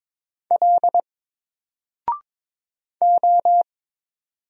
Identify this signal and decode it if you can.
Morse code, heard as a keyed tone, about 22 wpm, 700 Hz. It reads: L O
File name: Single-Letter-Number-22wpm